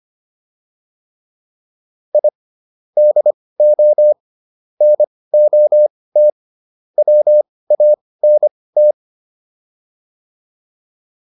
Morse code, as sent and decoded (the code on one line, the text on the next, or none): .. / -.. --- / -. --- - / .-- .- -. -
I DO NOT WANT